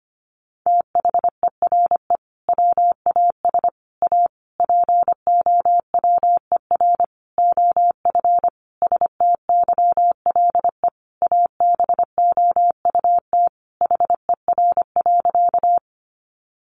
Morse code, heard as a keyed tone, about 25 wpm, 700 Hz. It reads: THERE WAS A POWER OF STYLE ABOUT HER.
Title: Sentences from Top 800 Words 25wpm